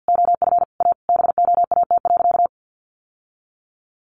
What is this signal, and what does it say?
Morse code, heard as a keyed tone, about 50 wpm, 700 Hz. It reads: OF A BOAT.
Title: Sentences from Top 1000 Words 50wpm